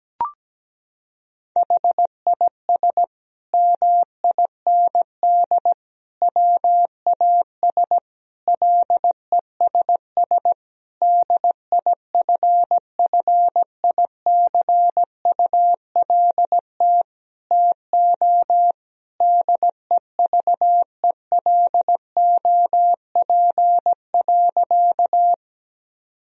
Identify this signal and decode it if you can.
Morse code, heard as a keyed tone, about 17 wpm, 700 Hz. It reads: HIS MIND WAS LESS DIFFICULT TO DEVELOP.
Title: Sentences from Top 1000 Words - Review Part 2 - 17wpm